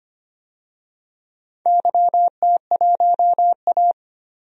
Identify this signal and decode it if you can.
Morse code, heard as a keyed tone, about 25 wpm, 690 Hz. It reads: YT1A
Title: DXCC-100 Call Signs - 4 Characters - 25wpm